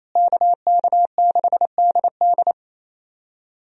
Morse code, heard as a keyed tone, about 28 wpm, 700 Hz. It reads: KK6DD
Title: USA Call Signs - 5 Characters - Repeated 2x - 28wpm